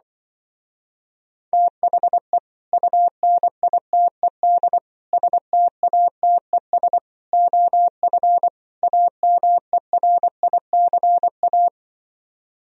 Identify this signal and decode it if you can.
Morse code, heard as a keyed tone, about 24 wpm, 700 Hz. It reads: THE UNITED STATES OF AMERICA
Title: Speed-Racing - Sets of 5 Words 20wpm